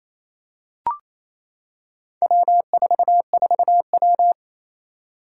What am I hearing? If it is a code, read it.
Morse code, heard as a keyed tone, about 28 wpm, 700 Hz. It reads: W44W
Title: Four Challenging Letter-Number Combinations 28wpm